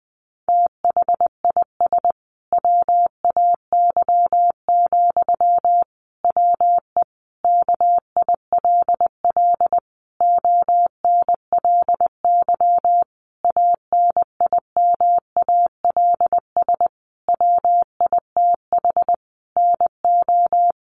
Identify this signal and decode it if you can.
Morse code, heard as a keyed tone, about 20 wpm, 700 Hz. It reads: THIS WAY, WE KILL ONLY ANIMALS WITH NO
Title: A Sound Of Thunder 20wpm